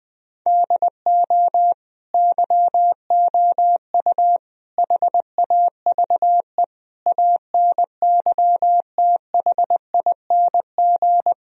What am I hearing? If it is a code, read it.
Morse code, heard as a keyed tone, about 20 wpm, 700 Hz. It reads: DO YOU HAVE ANYTHING
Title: Sentences from Top 1000 Words - Review Part 1 - 20wpm